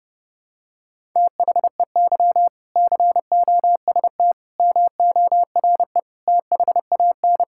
Morse code, heard as a keyed tone, about 30 wpm, 700 Hz. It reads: THEY COST MORE THAN
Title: Sentences from Top 1000 Words - Review Part 5 - 30wpm